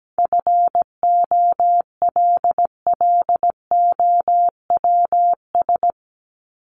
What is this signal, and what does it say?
Morse code, heard as a keyed tone, about 17 wpm, 700 Hz. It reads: FOLLOWS
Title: ES:S Suffix - Top 100 Words with Base Words 17wpm